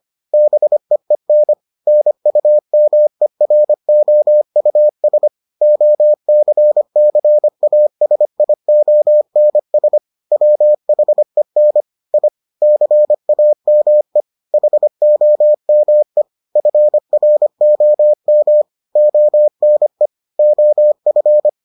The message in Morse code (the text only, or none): BEEN NUMEROUS OCCASIONS WHEN I CAME HOME FROM ONE OF